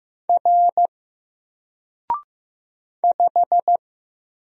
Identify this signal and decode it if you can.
Morse code, heard as a keyed tone, about 15 wpm, 700 Hz. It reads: R 5
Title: Single Letter-Number Alt 15wpm